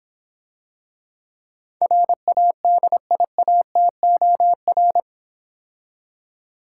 Morse code, heard as a keyed tone, about 26 wpm, 700 Hz. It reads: RADIATOR